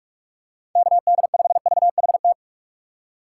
Morse code, heard as a keyed tone, about 44 wpm, 700 Hz. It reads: KD5VHT
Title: USA Call Signs - 6 Characters - 3x Character Spacing - 45wpm